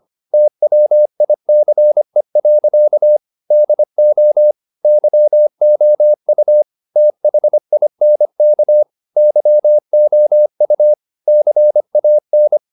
Morse code, heard as a keyed tone, about 25 wpm, 600 Hz. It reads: TWICE. DO YOU THINK YOU CAN
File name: the old man and the sea part 1 25wpm